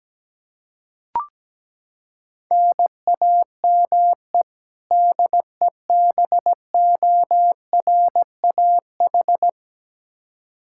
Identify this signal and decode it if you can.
Morse code, heard as a keyed tone, about 17 wpm, 700 Hz. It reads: NAME DEBORAH
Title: Practice QSO - Exchange 1 - Beginning - 17wpm